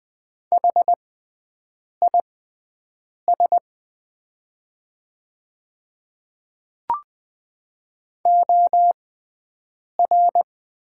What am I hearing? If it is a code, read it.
Morse code, heard as a keyed tone, about 20 wpm, spaced as for 7 wpm, 700 Hz. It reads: HIS OR
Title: 6x Character Spacing - Top 100 Words - 20wpm